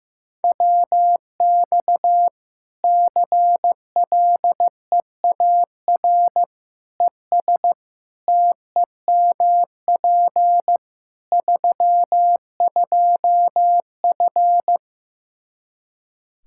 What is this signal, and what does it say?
Morse code, heard as a keyed tone, about 15 wpm, 700 Hz. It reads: WX CLEAR ES TEMP 32F
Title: Practice QSO - Exchange 2 - Info - 15wpm